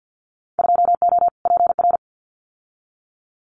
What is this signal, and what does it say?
Morse code, heard as a keyed tone, about 50 wpm, 700 Hz. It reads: 2O PR
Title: Winter Field Day Exchanges Encore 50wpm